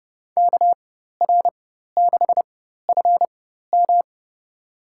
Morse code, heard as a keyed tone, about 30 wpm, spaced as for 14 wpm, 700 Hz. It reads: KR6FM
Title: USA Call Signs - 5 Characters - 4x Character Spacing - 30wpm